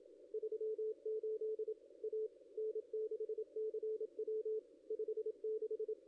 Morse code, heard as a keyed tone, about 26 wpm, 440 Hz. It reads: 38 A N6CW 56
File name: contests / ss-cw-k5tr-op-n5rz -1500